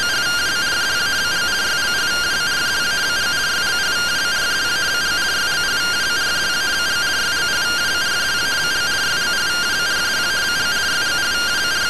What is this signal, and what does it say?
Two-tone teleprinter signal, about 170 Hz shift.
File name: findings / MR AXIAL 14.gen-09 rtty rx